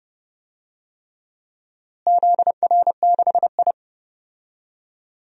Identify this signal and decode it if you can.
Morse code, heard as a keyed tone, about 31 wpm, 700 Hz. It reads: ZR6I
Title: DXCC-100 Call Signs - 4 Characters - Repeated 3x - 30wpm